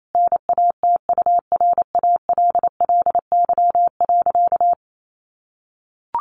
Morse code, heard as a keyed tone, about 28 wpm, 700 Hz. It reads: NATURALLY.
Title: Sentences from Top 1300 Words 28wpm